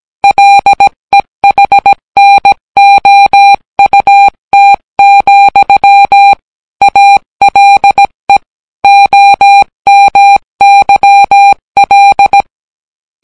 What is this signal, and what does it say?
Morse code, heard as a keyed tone, about 17 wpm, 800 Hz. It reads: LEHNOUT, ALE OMYL